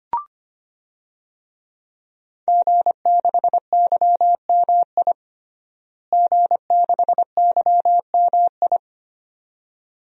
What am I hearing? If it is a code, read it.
Morse code, heard as a keyed tone, about 25 wpm, 700 Hz. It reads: G6YMI G6YMI
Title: DXCC-100 Call Signs - 5 Characters - Repeated 2x - 25wpm